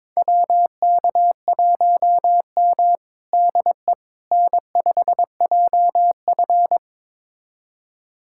Morse code, heard as a keyed tone, about 22 wpm, 700 Hz. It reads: WK1M DE N5JF
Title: USA Callsign DE Callsign - 4 Characters - 22wpm